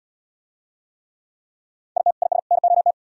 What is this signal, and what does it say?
Morse code, heard as a keyed tone, about 72 wpm, 700 Hz. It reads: IT IS NOT